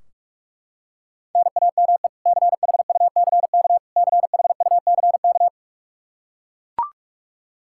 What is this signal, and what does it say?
Morse code, heard as a keyed tone, about 41 wpm, 700 Hz. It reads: NAME CHUCK CHUCK
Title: K1USN SST QSO Practice 45wpm